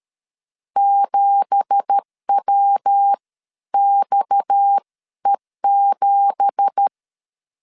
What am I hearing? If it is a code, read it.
Morse code, heard as a keyed tone, about 13 wpm, spaced as for 11 wpm, 800 Hz. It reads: 7WXE7
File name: CW / DET / 1 9 1500C S C 12wpm 3811